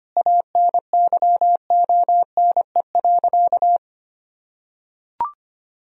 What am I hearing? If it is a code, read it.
Morse code, heard as a keyed tone, about 25 wpm, 700 Hz. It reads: ANYONE.
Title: Sentences from Top 1000 Words - Review Part 2 - 25wpm